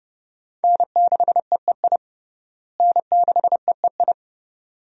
Morse code, heard as a keyed tone, about 30 wpm, 700 Hz. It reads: N6EEI N6EEI
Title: USA Call Signs - 5 Characters - Repeated 2x - 30wpm